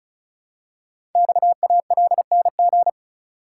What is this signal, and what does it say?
Morse code, heard as a keyed tone, about 36 wpm, 700 Hz. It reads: XALNG
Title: Code Groups - 5 Letters - 35wpm